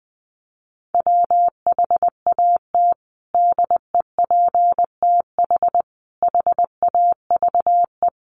Morse code, heard as a keyed tone, about 20 wpm, 700 Hz. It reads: WHAT DEPTH HAVE
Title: Sentences from Top 1200 Words 20wpm